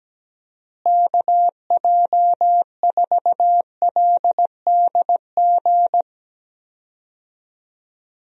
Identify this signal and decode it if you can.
Morse code, heard as a keyed tone, about 17 wpm, 700 Hz. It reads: KJ4LDG
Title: USA Call Signs - 6 Characters - Repeated 3x - 17wpm